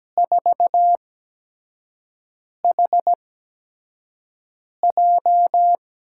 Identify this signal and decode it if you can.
Morse code, heard as a keyed tone, about 17 wpm, spaced as for 5 wpm, 700 Hz. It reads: 4HJ
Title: USA Call Signs - 5 Characters - 8x Character Spacing - 17wpm